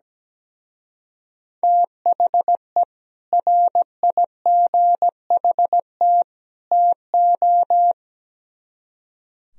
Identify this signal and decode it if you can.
Morse code, heard as a keyed tone, about 19 wpm, 700 Hz. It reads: THE RIGHT TO
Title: Speed-Racing - Sets of 3 Words 17wpm